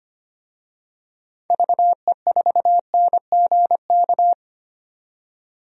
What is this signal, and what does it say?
Morse code, heard as a keyed tone, about 25 wpm, 700 Hz. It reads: VE4NGK
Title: DXCC-100 Call Signs - 6 Characters - 25wpm